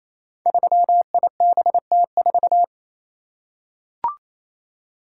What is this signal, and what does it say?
Morse code, heard as a keyed tone, about 29 wpm, 700 Hz. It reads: 3IBT4
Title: Code Groups - 5 Letter:Numbers - 28wpm